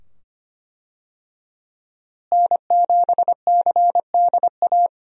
Morse code, heard as a keyed tone, about 25 wpm, 700 Hz. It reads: N7CDA
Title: USA Call Signs - 5 Characters - Repeated 2x - 25wpm